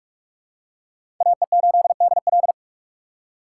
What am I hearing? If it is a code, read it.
Morse code, heard as a keyed tone, about 47 wpm, 700 Hz. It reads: AE8DL
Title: USA Call Signs - 5 Characters - 6x Character Spacing - 45wpm